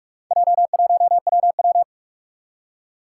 Morse code, heard as a keyed tone, about 46 wpm, 700 Hz. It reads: J1WW
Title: Four Challenging Letter-Number Combinations 45wpm